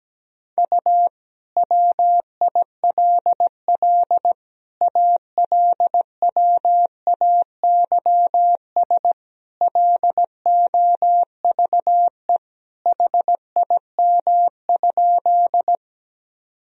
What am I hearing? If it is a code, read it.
Morse code, heard as a keyed tone, about 17 wpm, 700 Hz. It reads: U WILL ALWAYS LOVE HIM?
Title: Sentences from Top 400 Words and Abbreviations 17wpm